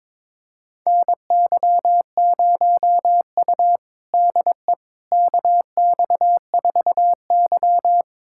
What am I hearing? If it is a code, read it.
Morse code, heard as a keyed tone, about 23 wpm, 700 Hz. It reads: NY0U DE KX4Y
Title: USA Callsign DE Callsign - 4 Characters - 22wpm